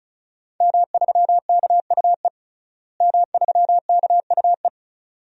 Morse code, heard as a keyed tone, about 36 wpm, 700 Hz. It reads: M3KUE M3KUE
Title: DXCC-100 Call Signs - 5 Characters - Repeated 2x - 35wpm